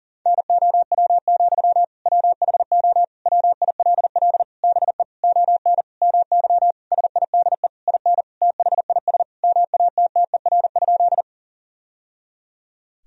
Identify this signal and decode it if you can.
Morse code, heard as a keyed tone, about 40 wpm, 700 Hz. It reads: NOW, WHO WILL BE ON MY SIDE IN THIS MATTER?
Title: Sentences from Top 1000 Words - Review Part 1 - 40wpm